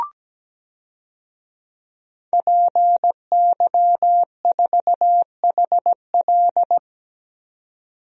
Morse code, heard as a keyed tone, about 17 wpm, 700 Hz. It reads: PY4HL
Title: DXCC-100 Call Signs - 5 Characters - 17wpm